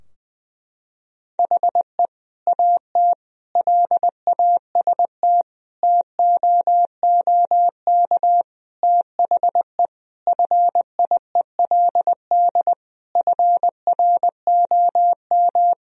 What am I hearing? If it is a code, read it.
Morse code, heard as a keyed tone, about 20 wpm, 700 Hz. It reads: HE AT LAST TOOK THE FIELD FROM
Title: Sentences from Top 1000 Words - Review Part 2 - 20wpm